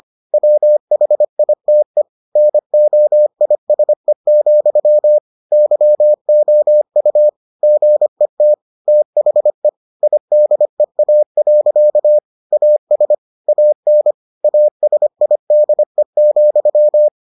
Morse code, heard as a keyed tone, about 25 wpm, 600 Hz. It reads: WHITE NOISE, YOU GET THE IDEA. AS AN ASIDE,